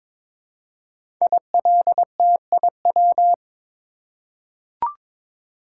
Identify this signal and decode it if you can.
Morse code, heard as a keyed tone, about 22 wpm, 700 Hz. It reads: ILTIW
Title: Code Groups - 5 Letters - 22wpm